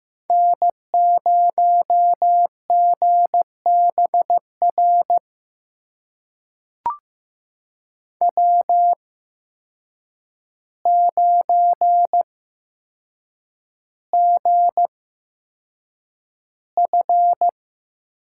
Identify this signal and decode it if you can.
Morse code, heard as a keyed tone, about 15 wpm, spaced as for 7 wpm, 700 Hz. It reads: N0GBR W9GF
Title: USA Call Signs - 8x Character Spacing - 15wpm